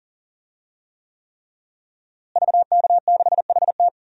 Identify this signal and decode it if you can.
Morse code, heard as a keyed tone, about 40 wpm, 700 Hz. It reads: VK6HT
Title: DXCC-100 Call Signs - 5 Characters - Repeated 3x - 40wpm